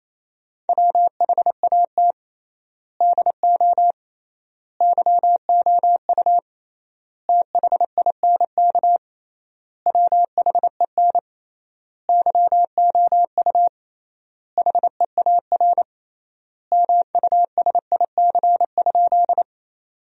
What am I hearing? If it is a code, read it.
Morse code, heard as a keyed tone, about 28 wpm, 700 Hz. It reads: WHAT DO YOU THINK WHEN YOU HEAR MUSIC?
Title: 3x Word-Spacing - Sentences from Top 300 Words - 28wpm